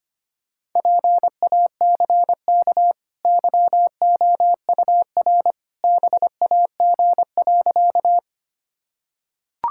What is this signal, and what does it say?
Morse code, heard as a keyed tone, about 25 wpm, 690 Hz. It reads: PACK YOUR BAG.
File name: Easy Sentences with Hints - 25wpm